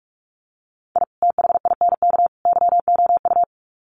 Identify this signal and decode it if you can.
Morse code, heard as a keyed tone, about 44 wpm, 700 Hz. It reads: I THINK YOU
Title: Sets of 3 Words 45wpm